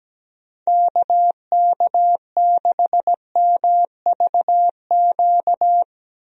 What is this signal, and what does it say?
Morse code, heard as a keyed tone, about 17 wpm, 700 Hz. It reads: KK6MVQ
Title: USA Call Signs - 6 Characters - Repeated 3x - 17wpm